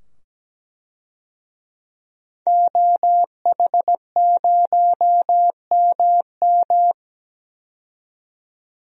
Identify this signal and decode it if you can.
Morse code, heard as a keyed tone, about 17 wpm, 700 Hz. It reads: OH0MM
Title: DXCC-100 Call Signs - 5 Characters - 17wpm